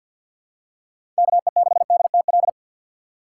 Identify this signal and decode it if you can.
Morse code, heard as a keyed tone, about 53 wpm, 700 Hz. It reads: KE6DTL
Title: USA Call Signs - 6 Characters - 4x Character Spacing - 50wpm